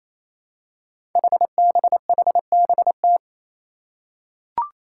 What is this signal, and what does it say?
Morse code, heard as a keyed tone, about 27 wpm, 700 Hz. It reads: HBHBT
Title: Five Challenging Letter-Number Combinations 28wpm